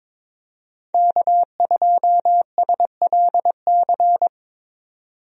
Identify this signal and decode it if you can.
Morse code, heard as a keyed tone, about 22 wpm, 700 Hz. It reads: K2SLC
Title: USA Call Signs - 5 Characters - 22wpm